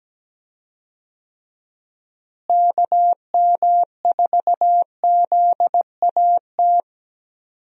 Morse code, heard as a keyed tone, about 17 wpm, 700 Hz. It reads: KM4ZAT
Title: USA Call Signs - 6 Characters - Repeated 3x - 17wpm